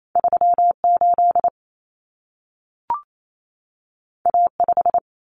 Morse code, heard as a keyed tone, about 28 wpm, 700 Hz. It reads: 38 A5
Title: Common Cut Numbers - 2-Digit 28wpm